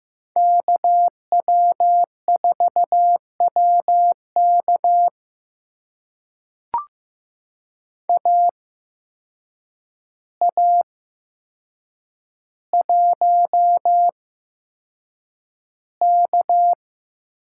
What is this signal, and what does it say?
Morse code, heard as a keyed tone, about 15 wpm, spaced as for 7 wpm, 700 Hz. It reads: KW4WK AA1K
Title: USA Call Signs - 5 Characters - 8x Character Spacing - 15wpm